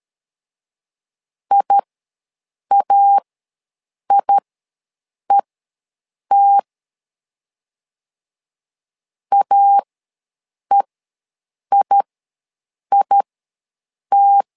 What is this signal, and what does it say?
Morse code, heard as a keyed tone, about 13 wpm, spaced as for 7 wpm, 800 Hz. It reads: IAIET AEIIT
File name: LEZIONI / 2 UHDN / 2 U ws 12wpm 1655